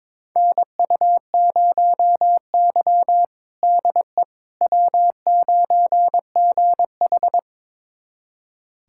Morse code, heard as a keyed tone, about 22 wpm, 700 Hz. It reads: NU0Y DE W9GH
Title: USA Callsign DE Callsign - 4 Characters - 22wpm